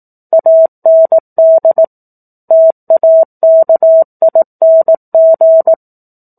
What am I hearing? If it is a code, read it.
Morse code, heard as a keyed tone, about 18 wpm, 650 Hz. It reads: AND TAKING
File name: 04-Chapter-04 18 WPM